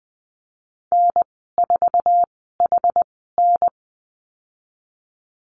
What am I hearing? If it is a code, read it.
Morse code, heard as a keyed tone, about 20 wpm, spaced as for 14 wpm, 700 Hz. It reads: N4HN
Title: 140 Lesson 23g - Letter f - Call Signs 20wpm